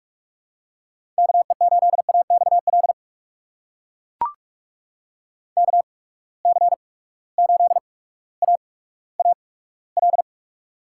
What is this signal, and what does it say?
Morse code, heard as a keyed tone, about 44 wpm, spaced as for 19 wpm, 700 Hz. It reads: KE8AXL KC8AAL
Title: USA Call Signs - 6 Characters - 8x Character Spacing - 45wpm